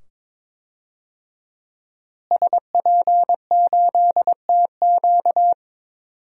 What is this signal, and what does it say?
Morse code, heard as a keyed tone, about 21 wpm, 700 Hz. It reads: SP8TQ
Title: DXCC-100 Call Signs - 5 Characters - Repeated 2x - 22wpm